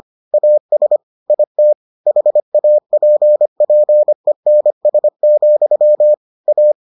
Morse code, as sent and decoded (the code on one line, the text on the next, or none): .- ... / .. - / .... .- .--. .--. -. ... --..-- / .-
AS IT HAPPENS, A